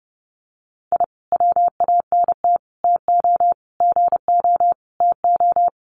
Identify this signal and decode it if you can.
Morse code, heard as a keyed tone, about 30 wpm, 700 Hz. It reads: I WANT TO GO TO
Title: Sets of 5 Words Encore 30wpm